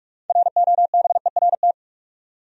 Morse code, heard as a keyed tone, about 41 wpm, 700 Hz. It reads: ROBERT
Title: USA Common Names - 45wpm